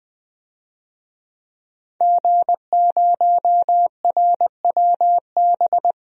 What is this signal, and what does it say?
Morse code, heard as a keyed tone, about 20 wpm, 700 Hz. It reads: G0RWB
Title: DXCC-100 Call Signs - 5 Characters - 20wpm